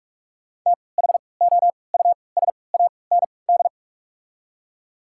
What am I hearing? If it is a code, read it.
Morse code, heard as a keyed tone, about 43 wpm, spaced as for 25 wpm, 690 Hz. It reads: THOUSAND
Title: 3x Character Spacing - Top 500 Words - 45wpm